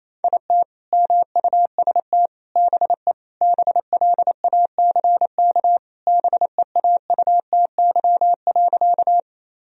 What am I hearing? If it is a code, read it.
Morse code, heard as a keyed tone, about 28 wpm, 700 Hz. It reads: IT MUST BE BLACK BEAUTY.
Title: Sentences from Top 1000 Words - Review Part 2 - 28wpm